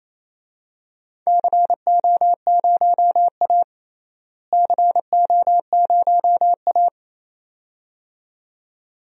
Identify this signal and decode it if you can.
Morse code, heard as a keyed tone, about 28 wpm, 700 Hz. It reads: CO0A CO0A
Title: DXCC-100 Call Signs - 4 Characters - Repeated 2x - 28wpm